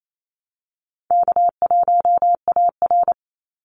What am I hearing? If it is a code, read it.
Morse code, heard as a keyed tone, about 28 wpm, 700 Hz. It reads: K1AR
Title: Call Signs - 28wpm